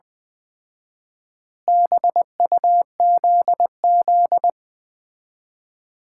Morse code, heard as a keyed tone, about 20 wpm, 700 Hz. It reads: BUZZ
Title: Common QSO Phrases - 4x Word-Spacing - 20wpm